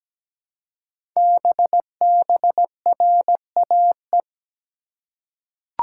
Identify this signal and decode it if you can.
Morse code, heard as a keyed tone, about 17 wpm, 700 Hz. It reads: BBRAE
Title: Code Groups - 5 Letters - 17wpm